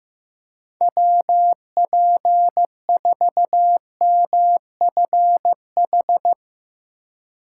Morse code, heard as a keyed tone, about 15 wpm, 700 Hz. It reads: WP4MFH
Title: USA Call Signs - 6 Characters - 4x Character Spacing - 15wpm